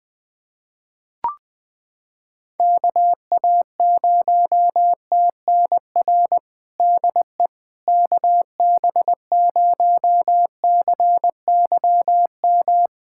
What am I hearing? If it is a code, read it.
Morse code, heard as a keyed tone, about 20 wpm, 700 Hz. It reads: KA0TNR DE KB0CYM
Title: USA Callsign DE Callsign 20wpm